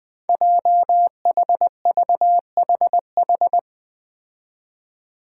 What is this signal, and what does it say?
Morse code, heard as a keyed tone, about 20 wpm, 700 Hz. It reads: JHVHH
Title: Five Challenging Letter-Number Combinations 20wpm